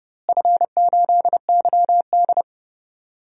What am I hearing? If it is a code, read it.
Morse code, heard as a keyed tone, about 30 wpm, 700 Hz. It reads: F8YD
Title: DXCC-100 Call Signs - 4 Characters - Repeated 2x - 30wpm